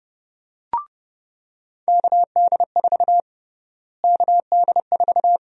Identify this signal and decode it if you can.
Morse code, heard as a keyed tone, about 30 wpm, 700 Hz. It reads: KD4 KD4
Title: USA Call Sign Prefixes - Repeated 2x - 30wpm